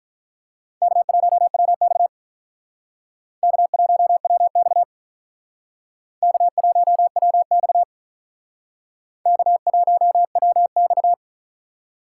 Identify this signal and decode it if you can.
Morse code, heard as a keyed tone, about 43 wpm, 700 Hz. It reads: K1WX K1WX K1WX K1WX
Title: Speed Racing - USA 1x2 Call Signs 35wpm